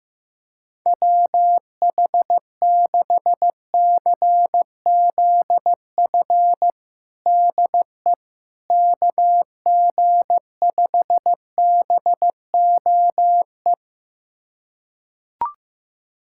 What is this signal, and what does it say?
Morse code, heard as a keyed tone, about 15 wpm, 700 Hz. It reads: WH6CZF DE KG5BOE
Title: USA Callsign DE Callsign - 6 Characters - 15wpm